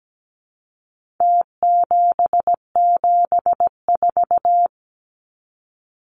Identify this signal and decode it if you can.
Morse code, heard as a keyed tone, about 17 wpm, 700 Hz. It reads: T774
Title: Numbers - Contest Style - 4-Digit 17wpm